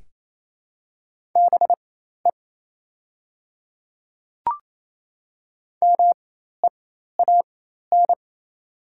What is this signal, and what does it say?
Morse code, heard as a keyed tone, about 28 wpm, spaced as for 13 wpm, 700 Hz. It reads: BE MEAN
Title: 4x Character Spacing - Top 200 Words - 28wpm